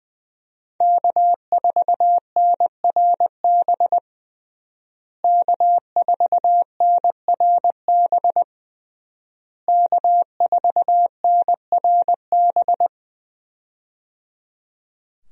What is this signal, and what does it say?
Morse code, heard as a keyed tone, about 21 wpm, 700 Hz. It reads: K4NRB K4NRB K4NRB
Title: USA Call Signs - 5 Characters - Repeated 3x - 20wpm